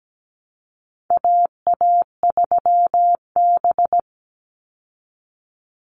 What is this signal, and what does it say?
Morse code, heard as a keyed tone, about 17 wpm, 700 Hz. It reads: AA3B
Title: Call Signs Repeated 3x - 17wpm